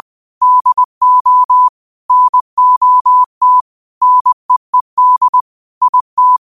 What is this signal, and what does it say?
Morse code, heard as a keyed tone, about 20 wpm, 1000 Hz. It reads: DO NOT NEED IT